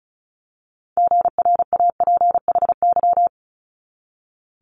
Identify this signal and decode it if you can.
Morse code, heard as a keyed tone, about 36 wpm, 700 Hz. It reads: GRAPHY
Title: Multi-letter Phonemes 35wpm